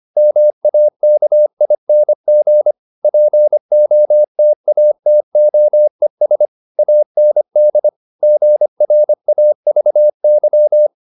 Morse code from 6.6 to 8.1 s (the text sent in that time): AND